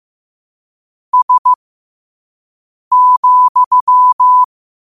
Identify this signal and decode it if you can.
Morse code, heard as a keyed tone, about 15 wpm, spaced as for 5 wpm, 1000 Hz. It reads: S,
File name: qotdfiles / 5wpm